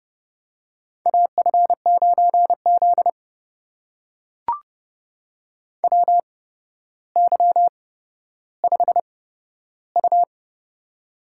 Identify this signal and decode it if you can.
Morse code, heard as a keyed tone, about 28 wpm, spaced as for 13 wpm, 700 Hz. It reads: AF9Z WY5U
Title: USA Call Signs - 4 Characters - 8x Character Spacing - 30wpm